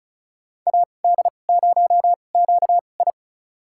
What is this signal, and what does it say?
Morse code, heard as a keyed tone, about 36 wpm, spaced as for 25 wpm, 700 Hz. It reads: AD0QI
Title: USA Call Signs - 5 Characters - 2x Character Spacing - 35wpm